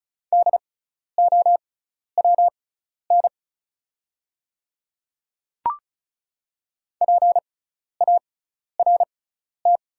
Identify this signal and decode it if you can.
Morse code, heard as a keyed tone, about 36 wpm, spaced as for 12 wpm, 700 Hz. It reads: DOWN PART
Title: 6x Character Spacing - Top 200 Words - 35wpm